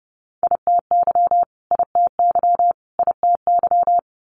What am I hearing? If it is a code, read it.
Morse code, heard as a keyed tone, about 30 wpm, 700 Hz. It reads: ITY ITY ITY
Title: Multi-letter Phonemes Repeated 3x 30wpm